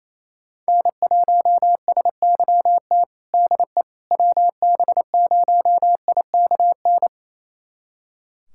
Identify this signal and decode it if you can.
Morse code, heard as a keyed tone, about 28 wpm, 700 Hz. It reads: N1SYT DE WB0IKN
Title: USA Callsign DE Callsign 28wpm